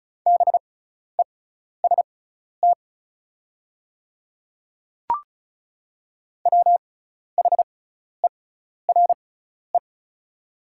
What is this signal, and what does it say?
Morse code, heard as a keyed tone, about 34 wpm, spaced as for 12 wpm, 700 Hz. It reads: BEST WHERE
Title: 6x Character Spacing - Top 500 Words - 35wpm